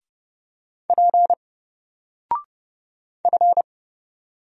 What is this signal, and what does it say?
Morse code, heard as a keyed tone, about 30 wpm, 700 Hz. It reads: P F
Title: Single Character (Letter:Number:Punctuation:Prosign) Alt 30wpm